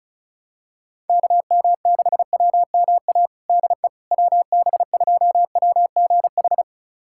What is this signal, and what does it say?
Morse code, heard as a keyed tone, about 36 wpm, 700 Hz. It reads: KM6WMA DE WB2WGH
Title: USA Callsign DE Callsign - 6 Characters - 35wpm